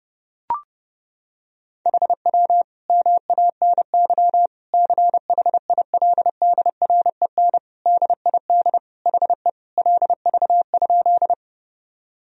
Morse code, heard as a keyed tone, about 30 wpm, 700 Hz. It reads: HW MANY CHILDREN DID HE LV?
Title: Sentences from Top 400 Words and Abbreviations 30wpm